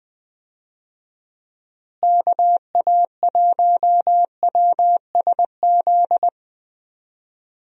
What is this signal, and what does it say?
Morse code, heard as a keyed tone, about 21 wpm, 700 Hz. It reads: KA1WSZ
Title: USA Call Signs - 6 Characters - Repeated 2x - 20wpm